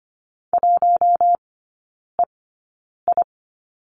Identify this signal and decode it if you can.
Morse code, heard as a keyed tone, about 25 wpm, spaced as for 9 wpm, 690 Hz. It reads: 1EI
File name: Lesson 15e - Number 5 - Three Characters 25wpm